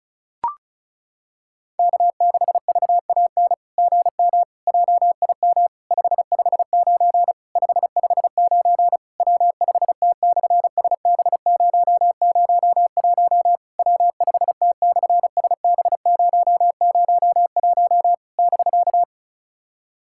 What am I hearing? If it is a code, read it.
Morse code, heard as a keyed tone, about 36 wpm, 700 Hz. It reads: K6VAN GM JIM 559 559 W5T/SB001 W5T/SB001 <BK>
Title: SOTA QSO Practice 35wpm